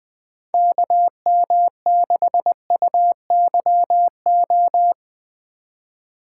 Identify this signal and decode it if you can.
Morse code, heard as a keyed tone, about 20 wpm, 700 Hz. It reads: KM6UYO
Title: USA Call Signs - 6 Characters - 2x Character Spacing - 20wpm